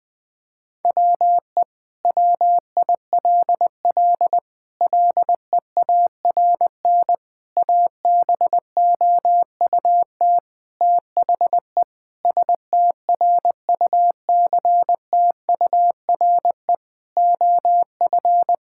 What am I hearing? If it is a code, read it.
Morse code, heard as a keyed tone, about 20 wpm, 700 Hz. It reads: WE WILL LEARN ABOUT THE STRUCTURE OF
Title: Sentences from Top 1000 Words - Review Part 6 - 20wpm